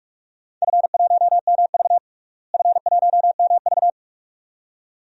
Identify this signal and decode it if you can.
Morse code, heard as a keyed tone, about 41 wpm, 700 Hz. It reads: F1MV F1MV
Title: DXCC-100 Call Signs - 4 Characters - Repeated 2x - 45wpm